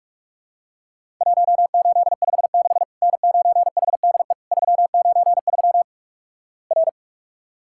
Morse code, heard as a keyed tone, about 46 wpm, 700 Hz. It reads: 1856 N9HDE 393 R
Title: Worked All Europe DX Contest - QTC Exchange - 45wpm